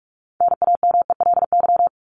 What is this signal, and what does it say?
Morse code, heard as a keyed tone, about 44 wpm, 700 Hz. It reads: NAMELY
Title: LY Suffix - Top 100 Words with Base Words 45wpm